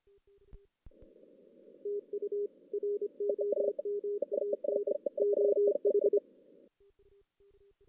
Morse code, heard as a keyed tone, about 25 wpm, 410 Hz. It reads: TU ROMAN OH
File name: 2016 JAN CW NAQP / 2016 Jan naqp k5ot - 0-5